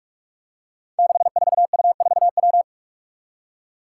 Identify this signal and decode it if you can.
Morse code, heard as a keyed tone, about 41 wpm, 700 Hz. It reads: 64U4W
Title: Five Challenging Letter-Number Combinations 45wpm